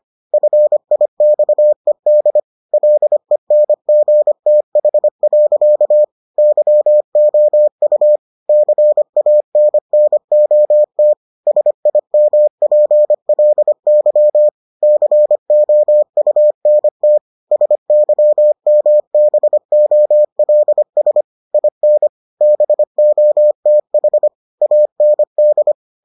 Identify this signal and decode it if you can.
Morse code, heard as a keyed tone, about 25 wpm, 600 Hz. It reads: FIXED LENGTH. YOU CANNOT SIMPLY COUNT SYMBOLS IN BOTH AND